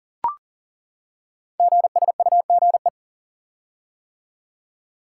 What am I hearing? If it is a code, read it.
Morse code, heard as a keyed tone, about 40 wpm, 690 Hz. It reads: GSUGE
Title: Code Groups - 5 Letters - 40wpm